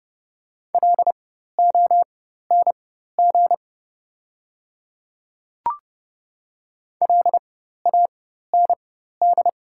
Morse code, heard as a keyed tone, about 30 wpm, spaced as for 14 wpm, 700 Hz. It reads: LONG LAND
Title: 4x Character Spacing - Top 300 Words - 30wpm